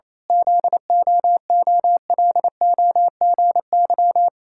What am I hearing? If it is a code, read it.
Morse code, heard as a keyed tone, about 28 wpm, 700 Hz. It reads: ZOOLOGY
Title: Pangram Challenge v2 - 28wpm